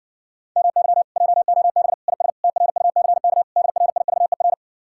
Morse code, heard as a keyed tone, about 60 wpm, 700 Hz. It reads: MY JOB IS TRUCK DRIVER